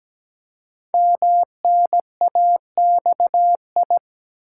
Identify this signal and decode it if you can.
Morse code, heard as a keyed tone, about 17 wpm, 700 Hz. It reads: MNAXI
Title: Code Groups - 5 Letters - 17wpm